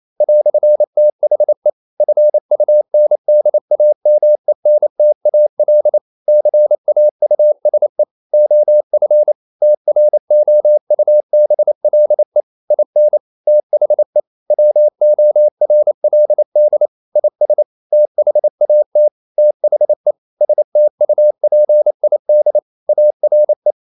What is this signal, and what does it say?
Morse code, heard as a keyed tone, about 28 wpm, 600 Hz. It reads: "THE FUNDAMENTAL CAUSE OF TROUBLE IN THE WORLD IS THAT THE STUPID ARE
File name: CW / Practice / EnglishQuotations0128